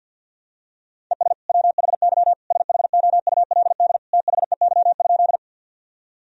Morse code, heard as a keyed tone, about 52 wpm, 700 Hz. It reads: ES WHY SHOULD THEY?
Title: Sentences from Top 300 Words and Abbreviations 50wpm